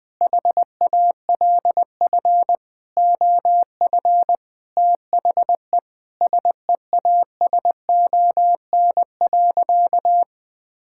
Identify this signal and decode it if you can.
Morse code, heard as a keyed tone, about 20 wpm, 700 Hz. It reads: HALF OF THE SEASON.
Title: Sentences with 7-Word Phrases - 20wpm